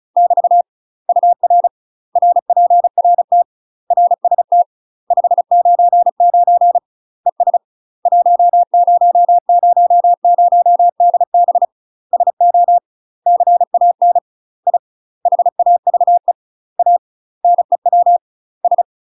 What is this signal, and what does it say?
Morse code, heard as a keyed tone, about 36 wpm, 700 Hz. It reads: <BT> UR RPRT RST 599 ES 1000DB SO CAN I HAVE A NEW S
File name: UK MASTER QSOs TRAINING / 35WPM / UK MASTER QSOs 0017